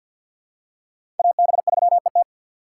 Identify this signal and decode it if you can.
Morse code, heard as a keyed tone, about 49 wpm, 700 Hz. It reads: AB3ET
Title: USA Call Signs - 5 Characters - 4x Character Spacing - 50wpm